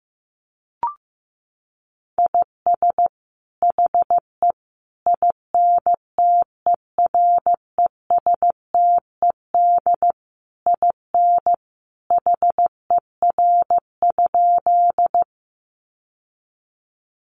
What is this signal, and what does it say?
Morse code, heard as a keyed tone, about 15 wpm, 700 Hz. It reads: IS HE INTERESTED IN HER?
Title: Sentences from Top 900 Words 15wpm